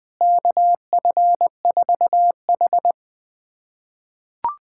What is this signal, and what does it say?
Morse code, heard as a keyed tone, about 20 wpm, 700 Hz. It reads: KF4H
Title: USA Callsign DE Callsign - 4 Characters - 20wpm